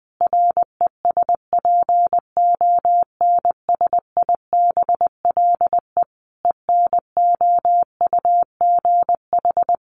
Morse code, heard as a keyed tone, about 19 wpm, 700 Hz. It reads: RESPONSIBLE ENOUGH
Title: Additional Easy Phrases 20wpm